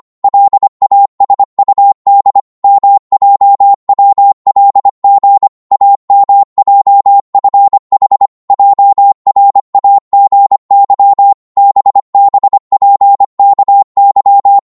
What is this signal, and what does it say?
Morse code, heard as a keyed tone, about 25 wpm, 800 Hz. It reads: LASUD MJWLG AMJFH JRAGY BBPKY